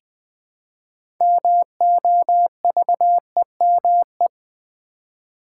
Morse code, heard as a keyed tone, about 20 wpm, 700 Hz. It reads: MOVEME
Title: Build A Word - 3 Letters At A Time - 8-Letter Words - 20wpm